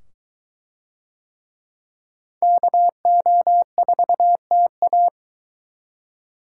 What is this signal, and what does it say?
Morse code, heard as a keyed tone, about 22 wpm, 700 Hz. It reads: KO4TA